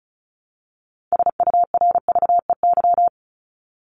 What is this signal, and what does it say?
Morse code, heard as a keyed tone, about 34 wpm, 700 Hz. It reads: SURVEY
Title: Top 1501-1750 Words 35wpm